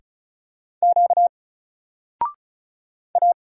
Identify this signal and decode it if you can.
Morse code, heard as a keyed tone, about 36 wpm, 700 Hz. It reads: Q A
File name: Single-Letter-Number-35wpm